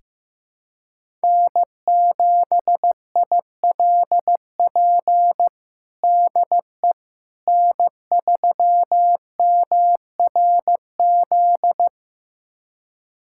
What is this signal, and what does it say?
Morse code, heard as a keyed tone, about 15 wpm, 700 Hz. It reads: N7ILP DE N3MRZ
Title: USA Callsign DE Callsign - 5 Characters - 15wpm